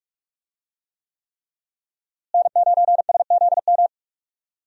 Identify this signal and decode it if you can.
Morse code, heard as a keyed tone, about 46 wpm, 700 Hz. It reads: N9SZM
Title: USA Call Signs - 5 Characters - Repeated 3x - 45wpm